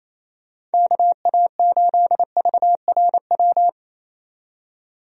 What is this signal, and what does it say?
Morse code, heard as a keyed tone, about 28 wpm, 700 Hz. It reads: KA8VRW
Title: USA Call Signs - 6 Characters - 28wpm